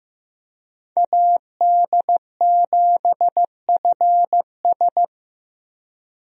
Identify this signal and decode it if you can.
Morse code, heard as a keyed tone, about 15 wpm, 700 Hz. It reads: AD7FS
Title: USA Call Signs - 5 Characters - Repeated 2x - 15wpm